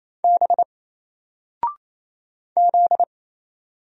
Morse code, heard as a keyed tone, about 28 wpm, 700 Hz. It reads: B Z
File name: Single-Letter-Number-28wpm